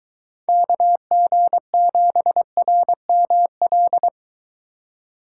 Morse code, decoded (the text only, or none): KG7RML